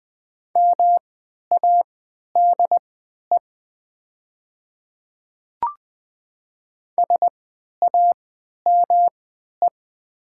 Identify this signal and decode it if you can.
Morse code, heard as a keyed tone, about 20 wpm, spaced as for 11 wpm, 700 Hz. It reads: MADE SAME
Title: 3x Character Spacing - Top 200 Words - 20wpm